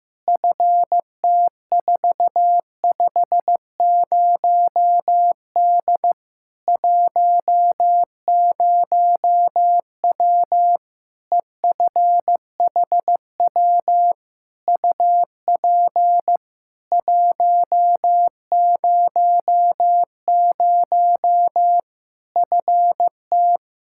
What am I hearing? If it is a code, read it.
Morse code, heard as a keyed tone, about 16 wpm, 700 Hz. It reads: FT450D 10W EFHW UP 100 FT
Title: K1USN SST QSO Practice 15wpm